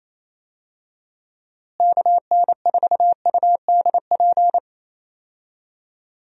Morse code, heard as a keyed tone, about 28 wpm, 700 Hz. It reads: KN4UDP
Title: USA Call Signs - 6 Characters - Repeated 3x - 28wpm